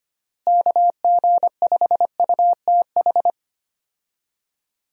Morse code, heard as a keyed tone, about 25 wpm, 700 Hz. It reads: KG5UTH
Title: USA Call Signs - 6 Characters - Repeated 3x - 25wpm